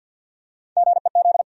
Morse code, heard as a keyed tone, about 53 wpm, 700 Hz. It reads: GEZ
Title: USA Call Sign Postfixes - 3 Letters 50wpm